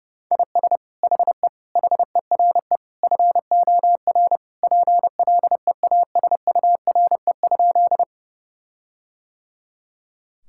Morse code, heard as a keyed tone, about 30 wpm, 700 Hz. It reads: IS HE HERE FOR PLEASURE?
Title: Sentences from Top 1600 Words 30wpm